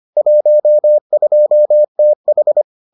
Morse code, heard as a keyed tone, about 25 wpm, 600 Hz. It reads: 12TH